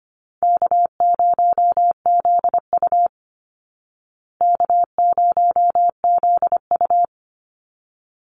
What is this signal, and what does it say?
Morse code, heard as a keyed tone, about 25 wpm, 700 Hz. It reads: K0ZU K0ZU
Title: Easy Call Signs Repeated 2x - 25wpm